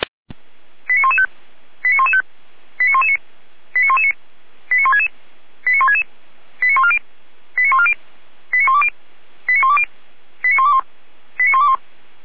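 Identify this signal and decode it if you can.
Five-tone selective call (ZVEI), heard as quick runs of five tones, tones of about 80 ms. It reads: 89115 89115 89119 89119 89166 89166 89140 89140 89120 89120 89121 89121
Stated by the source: ZVEI five-tone selective call 89067 received on SDR, from the feuerwehr-orlamuende.org fire brigade audio archive.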